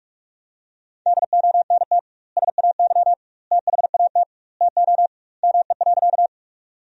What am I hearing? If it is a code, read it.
Morse code, heard as a keyed tone, about 48 wpm, 700 Hz. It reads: DONT SAY THAT TO ME.
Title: 2x Word-Spacing - Sentences from Top 300 Words - 45wpm